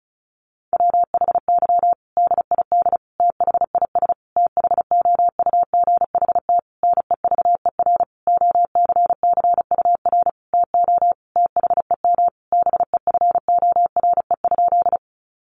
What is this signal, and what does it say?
Morse code, heard as a keyed tone, about 34 wpm, 700 Hz. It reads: WHY DID THIS THOUGHT NEVER OCCUR TO THEM BEFORE?
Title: Sentences from Top 900 Words 35wpm